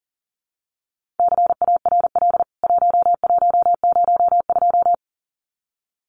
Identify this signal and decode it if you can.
Morse code, heard as a keyed tone, about 40 wpm, 700 Hz. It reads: CARL 1102
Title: CWT Exchange 40wpm